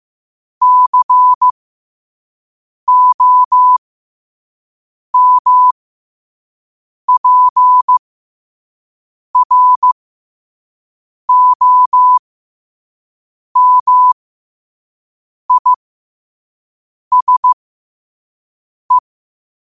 Morse code, heard as a keyed tone, about 15 wpm, spaced as for 5 wpm, 1000 Hz. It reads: COMPROMISE
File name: qotdfiles / 5wpm